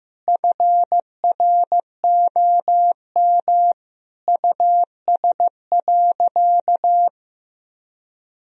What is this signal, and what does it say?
Morse code, heard as a keyed tone, about 16 wpm, 700 Hz. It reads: FROM US.
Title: Sentences from Top 1000 Words - Review Part 1 - 15wpm